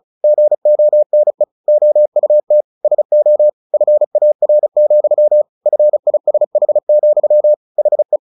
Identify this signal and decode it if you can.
Morse code, heard as a keyed tone, about 34 wpm, 600 Hz. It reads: GONE OUT SO FAR, FISH, HE
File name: the old man and the sea 2 35wpm